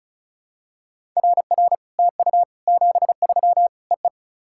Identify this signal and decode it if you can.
Morse code, heard as a keyed tone, about 36 wpm, 700 Hz. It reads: RR TU 73 EE
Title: SOTA QSO Practice 35wpm